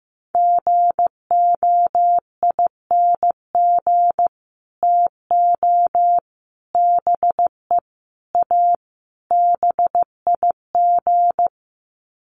Morse code, heard as a keyed tone, about 15 wpm, 700 Hz. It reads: GOING TO BE A BIG
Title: Sets of 5 Words Encore2 15wpm